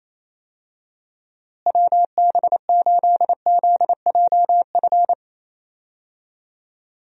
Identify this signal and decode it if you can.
Morse code, heard as a keyed tone, about 28 wpm, 700 Hz. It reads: WB8ZJF
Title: USA Call Signs - 6 Characters - Repeated 3x - 28wpm